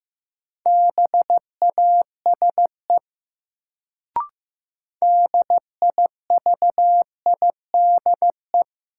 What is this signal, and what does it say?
Morse code, heard as a keyed tone, about 15 wpm, 700 Hz. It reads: BASE DIVIDE
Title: Sentences from Top 1600 Words - New Word Drill Alt - 15wpm